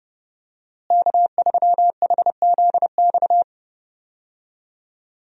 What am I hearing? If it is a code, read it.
Morse code, heard as a keyed tone, about 30 wpm, 700 Hz. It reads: K3HZX
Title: USA Call Signs - 5 Characters - Repeated 2x - 30wpm